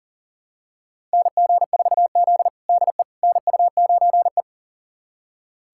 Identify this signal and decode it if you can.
Morse code, heard as a keyed tone, about 40 wpm, 700 Hz. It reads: NG4Z DE NU9E
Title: USA Callsign DE Callsign - 4 Characters - 40wpm